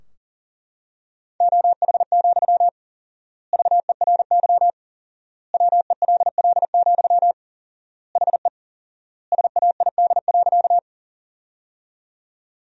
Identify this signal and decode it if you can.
Morse code, heard as a keyed tone, about 40 wpm, 700 Hz. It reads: OH, VERY WELL, HE SAID.
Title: 4x Word-Spacing - Sentences from Top 500 Words - 40wpm